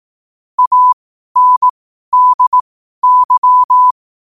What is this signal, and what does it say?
Morse code, heard as a keyed tone, about 18 wpm, spaced as for 12 wpm, 1000 Hz. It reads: ANDY
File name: qotdfiles / 12wpm